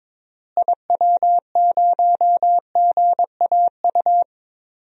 Morse code, heard as a keyed tone, about 22 wpm, 700 Hz. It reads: IW0GAU
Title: DXCC-100 Call Signs - 6 Characters - Repeated 3x - 22wpm